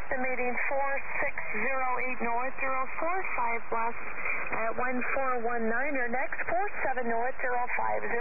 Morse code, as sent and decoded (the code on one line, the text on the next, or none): none